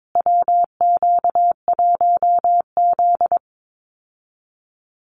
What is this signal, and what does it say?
Morse code, heard as a keyed tone, about 22 wpm, 700 Hz. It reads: WQ1Z
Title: USA Call Signs Repeated 2x 22wpm